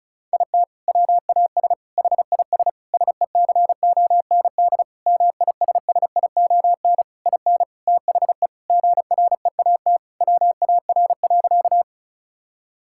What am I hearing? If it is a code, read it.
Morse code, heard as a keyed tone, about 36 wpm, 700 Hz. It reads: IT WAS HIS SECOND MISSION IN THE GREAT WAR.
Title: Sentences from Top 1600 Words 35wpm